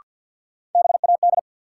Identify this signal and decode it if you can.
Morse code, heard as a keyed tone, about 48 wpm, 700 Hz. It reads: BAD
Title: Sentences from Top 700 Words - New Word Drill - 50wpm